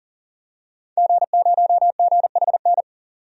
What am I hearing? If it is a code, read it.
Morse code, heard as a keyed tone, about 40 wpm, 700 Hz. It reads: G0GHN
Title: DXCC-100 Call Signs - 5 Characters - 40wpm